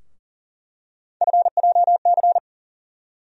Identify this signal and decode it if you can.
Morse code, heard as a keyed tone, about 40 wpm, 700 Hz. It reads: FJC
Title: USA Call Sign Postfixes - 3 Letters 40wpm